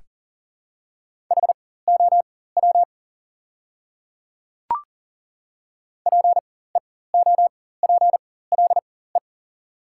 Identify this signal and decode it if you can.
Morse code, heard as a keyed tone, about 38 wpm, spaced as for 18 wpm, 700 Hz. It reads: HOW PEOPLE
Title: 4x Character Spacing - Top 100 Words - 40wpm